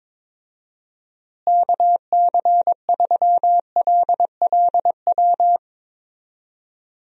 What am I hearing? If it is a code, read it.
Morse code, heard as a keyed tone, about 22 wpm, 700 Hz. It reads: KC3LLW
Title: USA Call Signs - 6 Characters - Repeated 3x - 22wpm